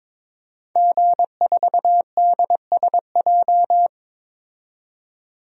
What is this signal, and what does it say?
Morse code, heard as a keyed tone, about 22 wpm, 700 Hz. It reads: G4DSJ
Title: DXCC-100 Call Signs - 5 Characters - 22wpm